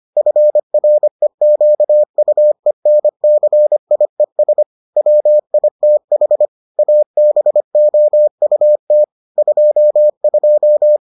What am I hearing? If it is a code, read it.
Morse code, heard as a keyed tone, about 25 wpm, 600 Hz. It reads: FREQUENCIES WITH ABOUT 22